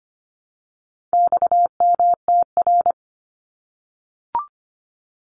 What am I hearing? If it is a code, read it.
Morse code, heard as a keyed tone, about 25 wpm, 700 Hz. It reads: XMTR
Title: Abbreviations Repeated 3x - 25wpm